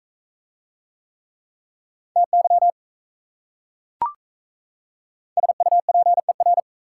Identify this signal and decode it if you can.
Morse code, heard as a keyed tone, about 35 wpm, 700 Hz. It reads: TY SUPER
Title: Speed-Racing Prefixes and Suffixes 28wpm